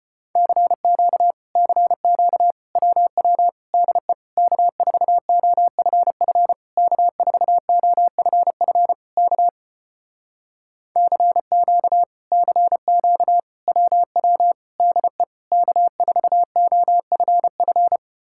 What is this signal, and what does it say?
Morse code, heard as a keyed tone, about 33 wpm, 700 Hz. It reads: CQ CQ WW DE K4OFF K4OFF K CQ CQ WW DE K4OFF